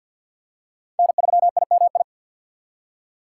Morse code, heard as a keyed tone, about 49 wpm, 700 Hz. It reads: N3IMI
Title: USA Call Signs - 4x Character Spacing - 50wpm